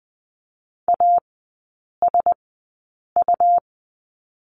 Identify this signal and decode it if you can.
Morse code, heard as a keyed tone, about 20 wpm, spaced as for 8 wpm, 700 Hz. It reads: ASU
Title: 172 Lesson 28e - Number 9 - Three Characters 20wpm